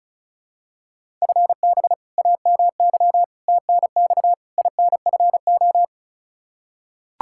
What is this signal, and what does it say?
Morse code, heard as a keyed tone, about 36 wpm, 700 Hz. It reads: FB AMY TNX INFO
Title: Practice QSO - Exchange 3 - More Info - 35wpm